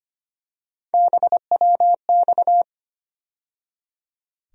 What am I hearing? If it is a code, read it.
Morse code, heard as a keyed tone, about 25 wpm, 700 Hz. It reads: BWX
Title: USA Call Sign Postfixes - 3 Letters 25wpm